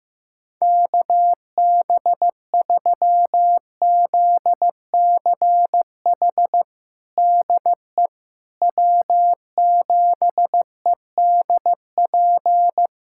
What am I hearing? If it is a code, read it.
Morse code, heard as a keyed tone, about 15 wpm, 700 Hz. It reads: KB3ZCH DE W7EDP
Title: USA Callsign DE Callsign 15wpm